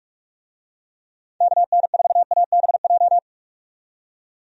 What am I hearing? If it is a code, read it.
Morse code, heard as a keyed tone, about 45 wpm, 700 Hz. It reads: KN4ABJ
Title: USA Call Signs - 6 Characters - 45wpm